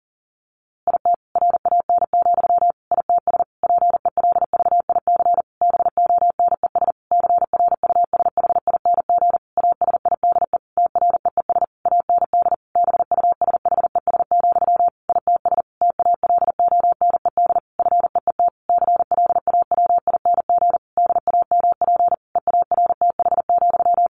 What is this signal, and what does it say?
Morse code, heard as a keyed tone, about 37 wpm, 700 Hz. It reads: IT RAN, ITS PELVIC BONES CRUSHING ASIDE TREES AND BUSHES, ITS TALONED FEET CLAWING DAMP EARTH,
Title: A Sound Of Thunder 40wpm